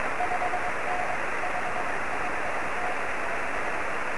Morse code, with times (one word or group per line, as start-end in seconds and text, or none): none